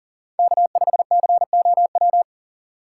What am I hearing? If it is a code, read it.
Morse code, heard as a keyed tone, about 37 wpm, 700 Hz. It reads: K5COW
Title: USA Call Signs - 5 Characters - 4x Character Spacing - 40wpm